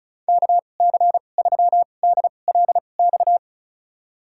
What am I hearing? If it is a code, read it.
Morse code, heard as a keyed tone, about 36 wpm, spaced as for 25 wpm, 700 Hz. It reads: KC3DLX
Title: USA Call Signs - 6 Characters - 2x Character Spacing - 35wpm